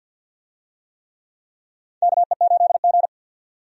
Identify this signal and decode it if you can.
Morse code, heard as a keyed tone, about 52 wpm, 700 Hz. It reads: KE8G
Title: USA Call Signs - 4 Characters - 50wpm